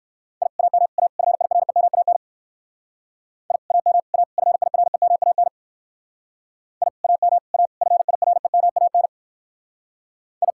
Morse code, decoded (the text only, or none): I AM A FIREMAN I AM A FIREMAN I AM A FIREMAN I